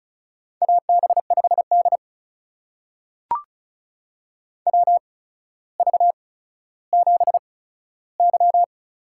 Morse code, heard as a keyed tone, about 36 wpm, spaced as for 15 wpm, 700 Hz. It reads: AB5D WV7Y
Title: USA Call Signs - 4 Characters - 8x Character Spacing - 35wpm